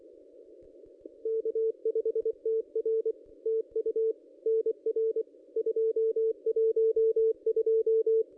Morse code, heard as a keyed tone, about 24 wpm, 440 Hz. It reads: K5TR TU NR 212
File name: contests / ss-cw-k5tr-op-n5rz -1400